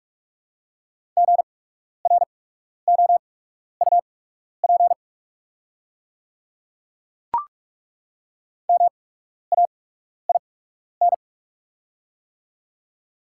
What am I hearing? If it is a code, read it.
Morse code, heard as a keyed tone, about 43 wpm, spaced as for 12 wpm, 690 Hz. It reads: GROUP MAIN
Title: 8x Character Spacing - Top 300 Words - 45wpm